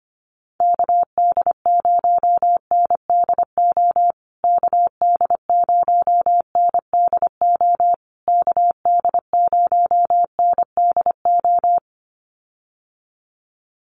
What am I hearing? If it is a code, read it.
Morse code, heard as a keyed tone, about 25 wpm, 700 Hz. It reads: KD0NDO KD0NDO KD0NDO
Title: USA Call Signs Repeated 3x 25wpm